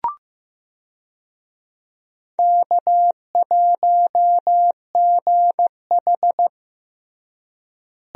Morse code, heard as a keyed tone, about 15 wpm, 700 Hz. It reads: K1GH
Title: USA Call Signs - 4 Characters - Repeated 3x - 15wpm